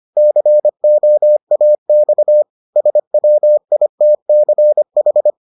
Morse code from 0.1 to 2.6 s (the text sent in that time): COAX